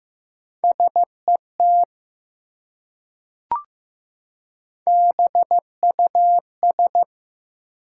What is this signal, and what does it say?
Morse code, heard as a keyed tone, about 15 wpm, 700 Hz. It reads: SET BUS
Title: Three Letter Words - 15wpm